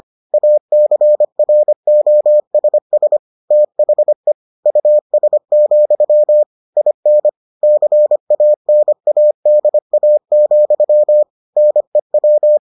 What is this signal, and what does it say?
Morse code, heard as a keyed tone, about 25 wpm, 600 Hz. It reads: ACROSS THE US, IN CANADA, NEW